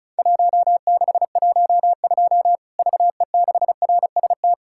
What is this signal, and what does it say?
Morse code, heard as a keyed tone, about 34 wpm, 700 Hz. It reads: 1612 VE6RST
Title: Worked All Europe DX Contest - QTC Exchange - 35wpm